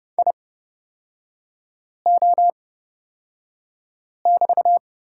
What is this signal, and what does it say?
Morse code, heard as a keyed tone, about 30 wpm, spaced as for 5 wpm, 700 Hz. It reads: IO<BT>
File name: Lesson 38e - Letter q - Three Characters 30wpm